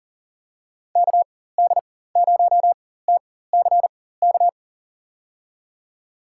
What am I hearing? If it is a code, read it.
Morse code, heard as a keyed tone, about 40 wpm, spaced as for 19 wpm, 700 Hz. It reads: KD0TCK
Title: USA Call Signs - 6 Characters - 4x Character Spacing - 40wpm